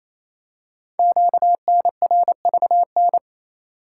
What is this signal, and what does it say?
Morse code, heard as a keyed tone, about 27 wpm, 700 Hz. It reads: QNRVN
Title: Code Groups - 5 Letters - 28wpm